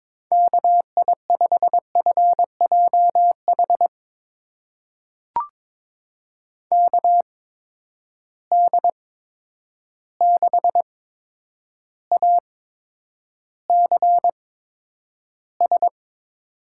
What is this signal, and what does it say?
Morse code, heard as a keyed tone, about 22 wpm, spaced as for 9 wpm, 700 Hz. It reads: KI5FJH KD6ACS
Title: USA Call Signs - 6 Characters - 8x Character Spacing - 22wpm